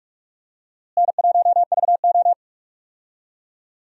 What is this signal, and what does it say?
Morse code, heard as a keyed tone, about 46 wpm, 700 Hz. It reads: N1VO
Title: USA Call Signs - 4 Characters - Repeated 2x - 45wpm